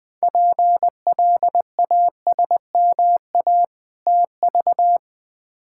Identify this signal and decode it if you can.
Morse code, heard as a keyed tone, about 20 wpm, 700 Hz. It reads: PLASMA TV
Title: Difficult Words - Speed-Racing with Hints - 20wpm